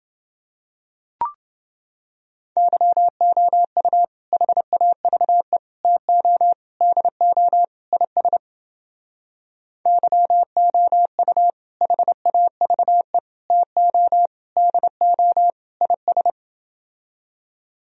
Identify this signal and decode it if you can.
Morse code, heard as a keyed tone, about 32 wpm, 700 Hz. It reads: YOU HAVE TO DO IS YOU HAVE TO DO IS